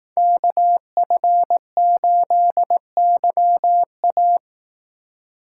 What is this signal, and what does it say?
Morse code, heard as a keyed tone, about 18 wpm, 700 Hz. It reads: KF8YA
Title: Speed Racing - USA Call Signs 15wpm